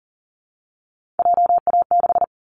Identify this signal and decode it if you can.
Morse code, heard as a keyed tone, about 40 wpm, 700 Hz. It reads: JA6
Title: Callsign Prefixes for Top 25 DXCC - Repeated 3x - 40wpm